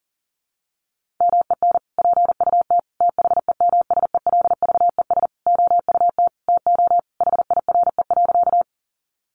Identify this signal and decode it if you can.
Morse code, heard as a keyed tone, about 40 wpm, 700 Hz. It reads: MEN PUT THEMSELVES OUT TO HIRE.
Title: Sentences from Top 1400 Words 40wpm